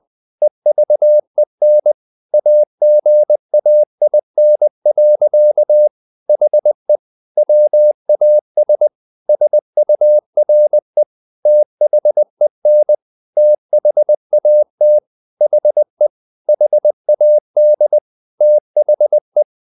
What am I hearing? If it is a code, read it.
Morse code, heard as a keyed tone, about 20 wpm, 600 Hz. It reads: EVEN AGAIN. HE WAS SURE THEN THAT HE HAD THE